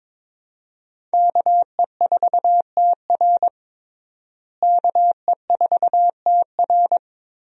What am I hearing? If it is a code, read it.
Morse code, heard as a keyed tone, about 22 wpm, 700 Hz. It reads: KE4TR KE4TR
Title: USA Call Signs - 5 Characters - Repeated 3x - 22wpm